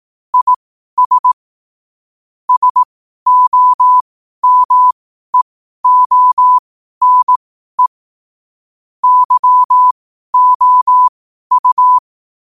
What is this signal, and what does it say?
Morse code, heard as a keyed tone, about 18 wpm, spaced as for 12 wpm, 1000 Hz. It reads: IS SOMEONE YOU
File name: qotdfiles / 12wpm